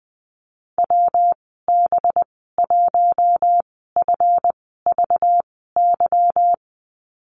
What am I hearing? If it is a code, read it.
Morse code, heard as a keyed tone, about 20 wpm, spaced as for 15 wpm, 700 Hz. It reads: WB1FVY
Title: 193 Lesson 31g - Letter v - Call Signs 20wpm